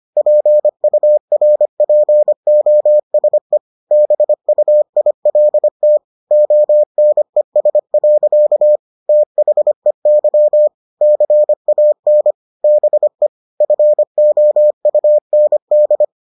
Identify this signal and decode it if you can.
Morse code, heard as a keyed tone, about 25 wpm, 600 Hz. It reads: PURPOSE BUILT ONES. THEY CAN BE FOUND